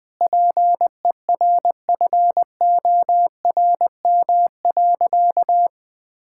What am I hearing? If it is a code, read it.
Morse code, heard as a keyed tone, about 20 wpm, 700 Hz. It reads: PERFORM.
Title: Sentences from Top 1000 Words - Review Part 5 - 20wpm